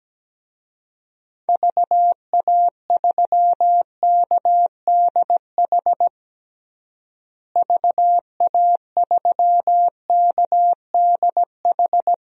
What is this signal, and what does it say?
Morse code, heard as a keyed tone, about 17 wpm, 700 Hz. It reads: VA3KDH VA3KDH
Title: DXCC-100 Call Signs - 6 Characters - Repeated 2x - 17wpm